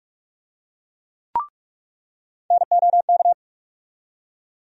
Morse code, heard as a keyed tone, about 47 wpm, 700 Hz. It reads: NOK
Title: USA Call Sign Postfixes - 3 Letters 45wpm